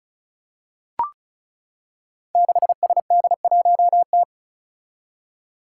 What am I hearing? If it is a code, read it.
Morse code, heard as a keyed tone, about 36 wpm, 700 Hz. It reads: 6SD1T
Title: Five Challenging Letter-Number Combinations 35wpm